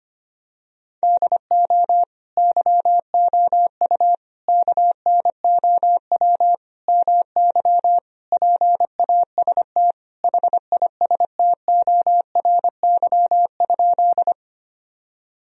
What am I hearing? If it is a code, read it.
Morse code, heard as a keyed tone, about 25 wpm, 700 Hz. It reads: DO YOU KNOW MY PAST HISTORY?
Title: Sentences from Top 1000 Words - Review Part 3 - 25wpm